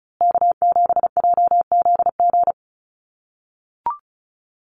Morse code, heard as a keyed tone, about 34 wpm, 700 Hz. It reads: K7JZG
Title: USA Call Signs Repeated 3x 35wpm